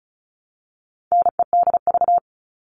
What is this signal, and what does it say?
Morse code, heard as a keyed tone, about 34 wpm, 700 Hz. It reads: NEDV
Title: All Cut Numbers - 4-Digit 35wpm